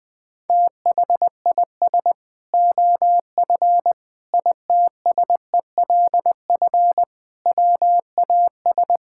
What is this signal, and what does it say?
Morse code, heard as a keyed tone, about 20 wpm, 700 Hz. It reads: THIS OF ITSELF WAS
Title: Sentences from Top 1000 Words - Review Part 5 - 20wpm